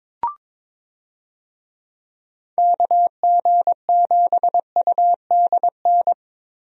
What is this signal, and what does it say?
Morse code, heard as a keyed tone, about 22 wpm, 700 Hz. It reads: KG7UDN
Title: USA Call Signs - 6 Characters - Repeated 3x - 22wpm